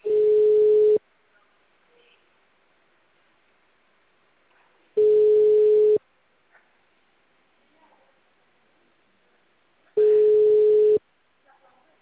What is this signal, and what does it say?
Ringback tone (Europe, 425 Hz), with speech.